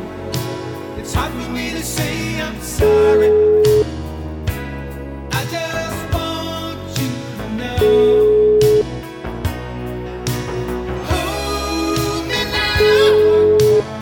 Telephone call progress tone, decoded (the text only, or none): ringback tone